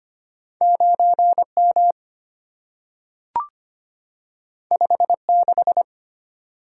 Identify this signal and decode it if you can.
Morse code, heard as a keyed tone, about 25 wpm, 700 Hz. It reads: 9M 56
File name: Two Letter-Number Combinations - 25wpm